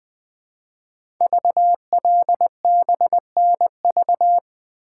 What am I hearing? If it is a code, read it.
Morse code, heard as a keyed tone, about 20 wpm, 700 Hz. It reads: VLBNV
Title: Code Groups - 5 Letter:Numbers - 20wpm